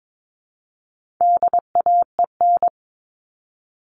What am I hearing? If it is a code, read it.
Morse code, heard as a keyed tone, about 22 wpm, 700 Hz. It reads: DAEN
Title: All Cut Numbers - 4-Digit 22wpm